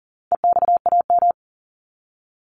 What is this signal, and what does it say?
Morse code, heard as a keyed tone, about 40 wpm, 700 Hz. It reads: EXAM
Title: Top 501-750 Words 40wpm